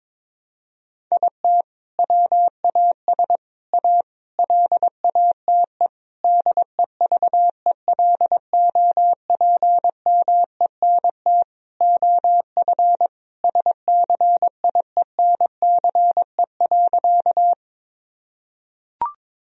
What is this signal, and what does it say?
Morse code, heard as a keyed tone, about 22 wpm, 700 Hz. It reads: IT WAS A LATE DEVELOPMENT OF SCIENCE.
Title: Sentences from Top 1000 Words - Review Part 4 - 22wpm